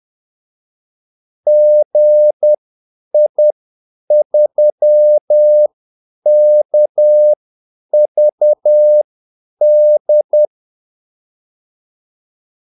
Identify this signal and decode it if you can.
Morse code, heard as a keyed tone, about 10 wpm, spaced as for 8 wpm, 600 Hz. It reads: GI3KVD
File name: Callsigns-10-8wpm